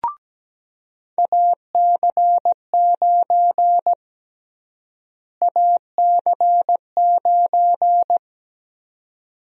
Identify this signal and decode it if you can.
Morse code, heard as a keyed tone, about 17 wpm, 700 Hz. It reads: AC9 AC9
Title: USA Call Sign Prefixes - Repeated 2x - 17wpm